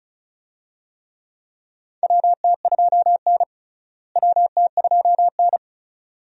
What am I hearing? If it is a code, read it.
Morse code, heard as a keyed tone, about 36 wpm, 700 Hz. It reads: WT2N WT2N
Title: USA Call Signs - 4 Characters - Repeated 2x - 35wpm